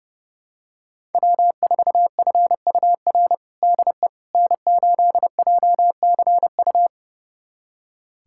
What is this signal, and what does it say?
Morse code, heard as a keyed tone, about 32 wpm, 700 Hz. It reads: W4FUR DE N8JCU
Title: USA Callsign DE Callsign - 5 Characters - 30wpm